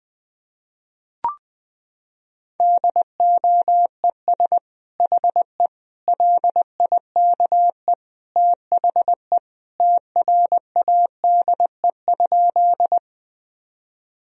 Morse code, heard as a keyed tone, about 20 wpm, 700 Hz. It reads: DOES HE LIKE THE TRADE?
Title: Sentences from Top 1000 Words - Review Part 4 - 20wpm